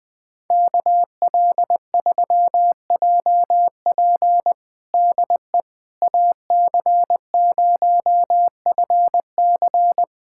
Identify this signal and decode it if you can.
Morse code, heard as a keyed tone, about 21 wpm, 700 Hz. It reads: KL3JP DE AC0FC
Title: USA Callsign DE Callsign - 5 Characters - 20wpm